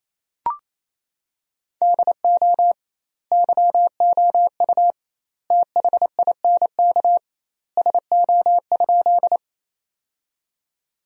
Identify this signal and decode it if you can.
Morse code, heard as a keyed tone, about 28 wpm, 700 Hz. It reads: DO YOU THINK SO?
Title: 2x Word-Spacing - Sentences from Top 200 Words - 28wpm